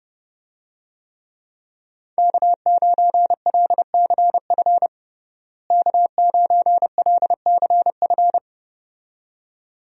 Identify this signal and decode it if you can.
Morse code, heard as a keyed tone, about 31 wpm, 700 Hz. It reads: K9LCF K9LCF
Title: USA Call Signs - 5 Characters - Repeated 2x - 30wpm